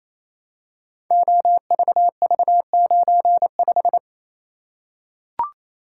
Morse code, heard as a keyed tone, about 28 wpm, 700 Hz. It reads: OVV95
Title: Code Groups - 5 Letter:Numbers - 28wpm